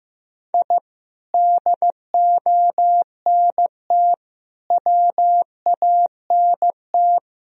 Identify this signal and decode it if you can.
Morse code, heard as a keyed tone, about 15 wpm, 700 Hz. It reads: I DONT WANT
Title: Sentences from Top 1000 Words - Review Part 4 - 15wpm